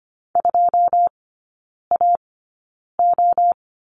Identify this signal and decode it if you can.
Morse code, heard as a keyed tone, about 25 wpm, spaced as for 9 wpm, 690 Hz. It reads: 2AO
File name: Lesson 14e - Number 2 - Three Characters 25wpm